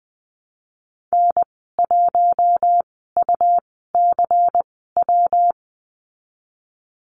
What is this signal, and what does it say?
Morse code, heard as a keyed tone, about 20 wpm, spaced as for 15 wpm, 700 Hz. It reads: N1UCW
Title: 161 Lesson 26g - Letter g - Call Signs 20wpm